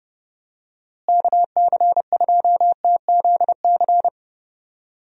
Morse code, heard as a keyed tone, about 30 wpm, 700 Hz. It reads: KC2TZC
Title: USA Call Signs - 6 Characters - Repeated 2x - 30wpm